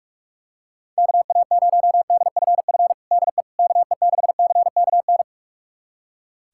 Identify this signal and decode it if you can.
Morse code, heard as a keyed tone, about 46 wpm, 700 Hz. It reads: KA0DFF DE KE6CKN
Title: USA Callsign DE Callsign 45wpm